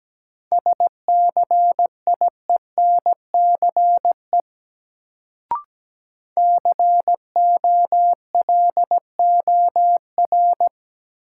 Morse code, heard as a keyed tone, about 17 wpm, 700 Hz. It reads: SCIENCE COLOR
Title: Sentences from Top 1000 Words - Review Part 2 - New Word Drill - 17wpm